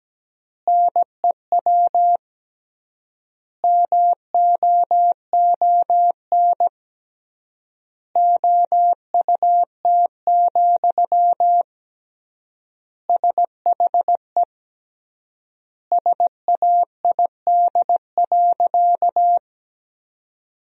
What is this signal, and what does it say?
Morse code, heard as a keyed tone, about 17 wpm, 700 Hz. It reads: NEW MOON OUT, SHE SAID.
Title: 3x Word-Spacing - Sentences from Top 500 Words - 17wpm